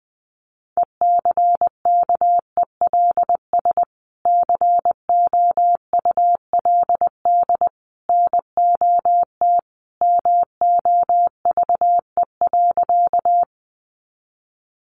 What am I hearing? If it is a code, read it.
Morse code, heard as a keyed tone, about 20 wpm, 700 Hz. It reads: ECKELS COULD NOT MOVE.
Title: A Sound Of Thunder 20wpm